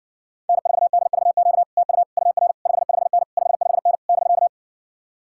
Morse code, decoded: N4DVQ TU UR 55N 55N <BK>

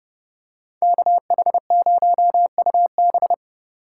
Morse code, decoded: KH0UB

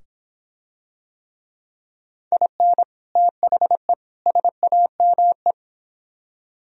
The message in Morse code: IN THE SAME

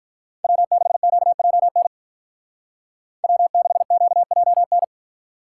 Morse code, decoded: W6QJN W6QJN

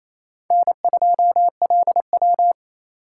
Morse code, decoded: N2LW

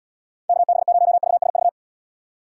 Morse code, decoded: KB0VSV

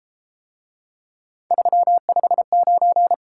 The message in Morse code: 359

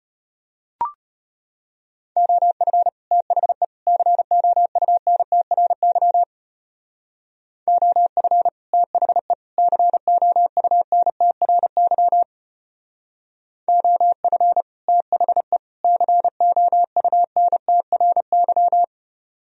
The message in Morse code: OF THE COUNTRY OF THE COUNTRY OF THE COUNTRY

